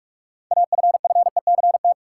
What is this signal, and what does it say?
Morse code, heard as a keyed tone, about 46 wpm, 700 Hz. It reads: AFFECT
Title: Six Letter Words - 45wpm